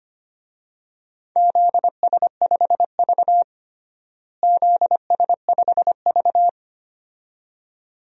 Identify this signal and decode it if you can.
Morse code, heard as a keyed tone, about 25 wpm, 700 Hz. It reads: ZS5V ZS5V
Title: DXCC-100 Call Signs - 4 Characters - Repeated 2x - 25wpm